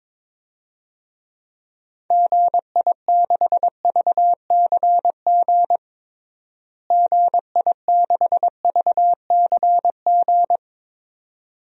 Morse code, heard as a keyed tone, about 23 wpm, 700 Hz. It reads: GI6VCG GI6VCG
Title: DXCC-100 Call Signs - 6 Characters - Repeated 2x - 22wpm